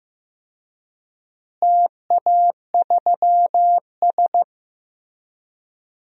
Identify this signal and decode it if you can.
Morse code, heard as a keyed tone, about 16 wpm, 700 Hz. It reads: TA3S
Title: DXCC-100 Call Signs - 4 Characters - Repeated 3x - 15wpm